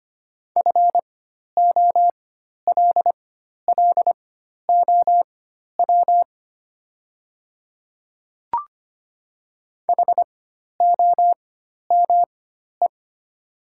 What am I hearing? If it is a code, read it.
Morse code, heard as a keyed tone, about 25 wpm, spaced as for 12 wpm, 700 Hz. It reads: FOLLOW HOME
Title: 4x Character Spacing - Top 200 Words - 25wpm